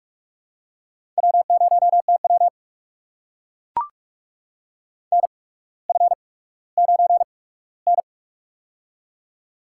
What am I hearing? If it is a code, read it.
Morse code, heard as a keyed tone, about 43 wpm, spaced as for 19 wpm, 700 Hz. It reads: W0TW NF9N
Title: USA Call Signs - 4 Characters - 8x Character Spacing - 45wpm